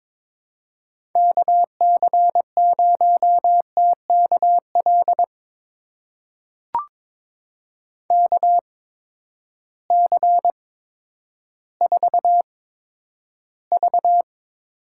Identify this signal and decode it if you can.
Morse code, heard as a keyed tone, about 21 wpm, spaced as for 11 wpm, 700 Hz. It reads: KC0TKL KC4V
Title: USA Call Signs - 6 Characters - 8x Character Spacing - 22wpm